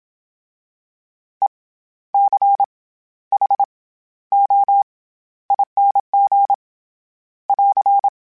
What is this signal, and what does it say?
Morse code, heard as a keyed tone, about 27 wpm, 800 Hz. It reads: E C H O ING <AR>
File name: Special INGs-mp3s / Special INGs-mp3s / SlowToFaster ing @25